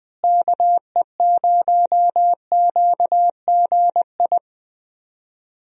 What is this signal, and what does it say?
Morse code, heard as a keyed tone, about 20 wpm, 700 Hz. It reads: KE0QGI
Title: USA Call Signs - 6 Characters - Repeated 2x - 20wpm